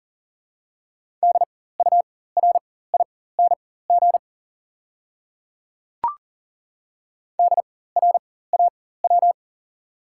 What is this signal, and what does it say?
Morse code, heard as a keyed tone, about 40 wpm, spaced as for 19 wpm, 700 Hz. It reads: DURING DRAW
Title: 4x Character Spacing - Top 500 Words - 40wpm